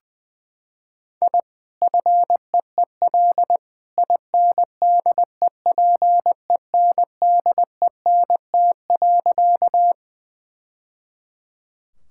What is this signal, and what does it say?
Morse code, heard as a keyed tone, about 20 wpm, 700 Hz. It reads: I FEEL INDEPENDENT.
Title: Sentences from Top 1000 Words - Review Part 6 - 20wpm